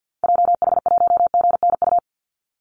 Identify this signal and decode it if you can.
Morse code, heard as a keyed tone, about 50 wpm, 700 Hz. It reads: JH1GNU
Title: Call Signs Repeated 2x - 50wpm